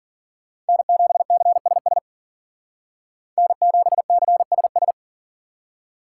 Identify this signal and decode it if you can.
Morse code, heard as a keyed tone, about 52 wpm, 700 Hz. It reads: N7CSS N7CSS